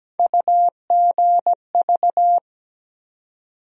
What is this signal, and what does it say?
Morse code, heard as a keyed tone, about 17 wpm, 700 Hz. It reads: UGV
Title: USA Call Sign Postfixes - 3 Letters 17wpm